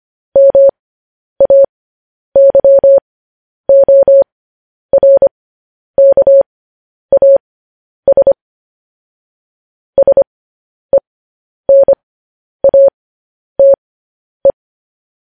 Morse code, heard as a keyed tone, about 25 wpm, spaced as for 10 wpm, 550 Hz. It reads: MAYORKAS SENATE